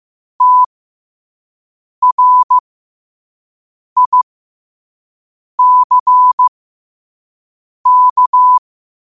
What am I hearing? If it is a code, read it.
Morse code, heard as a keyed tone, about 15 wpm, spaced as for 5 wpm, 1000 Hz. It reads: TRICK
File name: qotdfiles / 5wpm